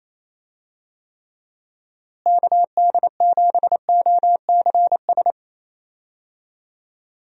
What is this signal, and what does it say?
Morse code, heard as a keyed tone, about 27 wpm, 700 Hz. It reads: KD7OCS
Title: USA Call Signs - 6 Characters - Repeated 3x - 28wpm